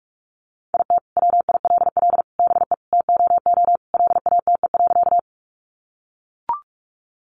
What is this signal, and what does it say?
Morse code, heard as a keyed tone, about 44 wpm, 700 Hz. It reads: IT WILL BE TOO LATE.
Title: Sentences from Top 300 Words 45wpm